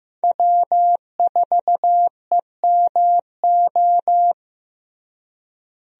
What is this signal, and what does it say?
Morse code, heard as a keyed tone, about 15 wpm, 700 Hz. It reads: W4EMO
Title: USA Call Signs - 5 Characters - Repeated 2x - 15wpm